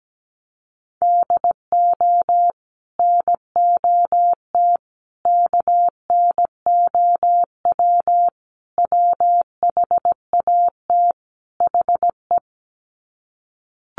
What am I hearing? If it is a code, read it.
Morse code, heard as a keyed tone, about 17 wpm, 700 Hz. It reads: DO NOT KNOW WHAT HE
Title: Sets of 5 Words Encore2 17wpm